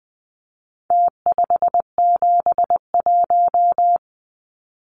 Morse code, heard as a keyed tone, about 20 wpm, 700 Hz. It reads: T571
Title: Numbers - Contest Style - 4-Digit 20wpm